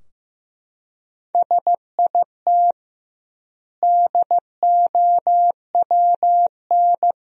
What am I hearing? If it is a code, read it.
Morse code, heard as a keyed tone, about 16 wpm, 700 Hz. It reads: SIT DOWN
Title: 2x Word-Spacing - Sentences from Top 500 Words - 15wpm